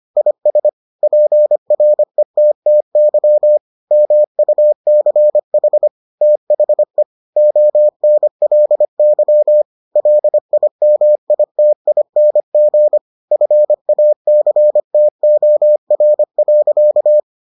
Morse code, heard as a keyed tone, about 25 wpm, 600 Hz. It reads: IS PRETTY MUCH THE ONLY LIMITING FACTOR.